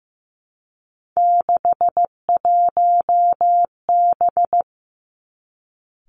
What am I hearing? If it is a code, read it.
Morse code, heard as a keyed tone, about 15 wpm, 700 Hz. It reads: 61B
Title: Three Challenging Letter-Number Combinations 15wpm